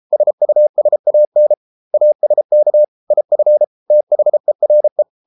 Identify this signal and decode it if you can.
Morse code, heard as a keyed tone, about 33 wpm, 600 Hz. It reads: SUSAN ASK IF THERE